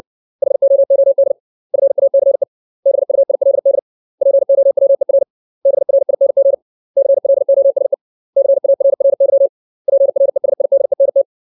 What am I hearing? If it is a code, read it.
Morse code, heard as a keyed tone, about 60 wpm, 550 Hz. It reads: HOOD FACE BUILD POWER BRING CLOSE CARRY PRESIDENT
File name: 632 words / 632 Words 60WPM